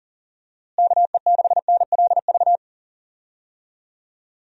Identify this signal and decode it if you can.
Morse code, heard as a keyed tone, about 40 wpm, 700 Hz. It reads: KE6NLV
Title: USA Call Signs - 6 Characters - Repeated 3x - 40wpm